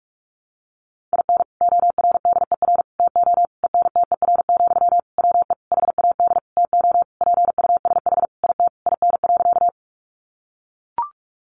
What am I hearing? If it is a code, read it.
Morse code, heard as a keyed tone, about 44 wpm, 700 Hz. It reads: IN ORDER TO ENTER, WE HAD TO PUSH IT IN.
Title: Sentences from Top 1300 Words 45wpm